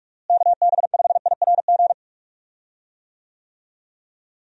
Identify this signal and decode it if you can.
Morse code, heard as a keyed tone, about 46 wpm, 700 Hz. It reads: KB5IRG
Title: USA Call Signs - 6 Characters - Repeated 3x - 45wpm